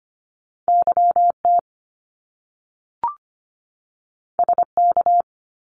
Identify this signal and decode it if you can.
Morse code, heard as a keyed tone, about 25 wpm, 700 Hz. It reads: YT SK
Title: Canadian Provinces and Territories 25wpm